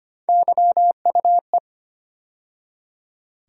Morse code, heard as a keyed tone, about 25 wpm, 700 Hz. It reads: YUE
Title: USA Call Sign Postfixes - 3 Letters 25wpm